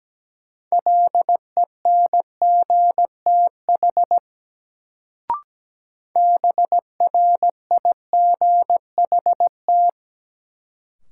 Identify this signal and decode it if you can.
Morse code, heard as a keyed tone, about 17 wpm, 700 Hz. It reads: LENGTH BRIGHT
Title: Sentences from Top 1000 Words - New Word Drill - 17wpm